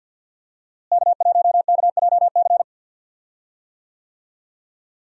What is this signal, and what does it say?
Morse code, heard as a keyed tone, about 47 wpm, 700 Hz. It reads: K1KJC
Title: USA Call Signs - 5 Characters - Repeated 3x - 50wpm